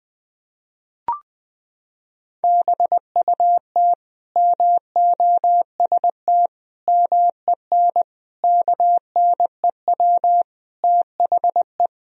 Morse code, heard as a keyed tone, about 20 wpm, 700 Hz. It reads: BUT MOST MEN KNEW THE
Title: Sentences from Top 1000 Words - Review Part 1 - 20wpm